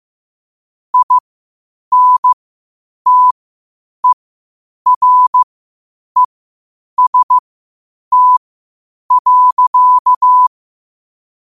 Morse code, heard as a keyed tone, about 15 wpm, spaced as for 8 wpm, 1000 Hz. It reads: INTEREST.
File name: qotdfiles / 8wpm